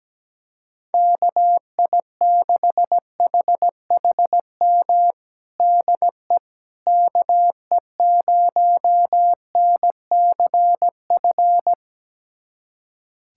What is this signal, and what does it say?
Morse code, heard as a keyed tone, about 17 wpm, 700 Hz. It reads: KI6HHM DE KE0NCF
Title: USA Callsign DE Callsign 17wpm